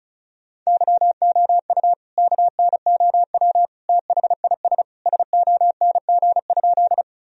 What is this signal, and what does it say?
Morse code, heard as a keyed tone, about 34 wpm, 700 Hz. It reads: YOU KNOW THIS SONG?
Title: Sentences from Top 1000 Words - Review Part 2 - 35wpm